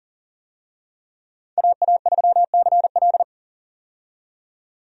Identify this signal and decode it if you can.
Morse code, heard as a keyed tone, about 37 wpm, 700 Hz. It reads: AA3CL
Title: USA Call Signs - 5 Characters - 40wpm